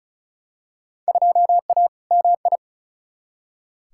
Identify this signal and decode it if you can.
Morse code, heard as a keyed tone, about 36 wpm, 700 Hz. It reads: 2A MI
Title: Field Day Exchanges - 35wpm